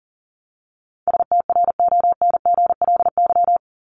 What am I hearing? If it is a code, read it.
Morse code, heard as a keyed tone, about 40 wpm, 700 Hz. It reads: STRONGLY
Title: LY Suffix - Top 100 Words with Base Words 40wpm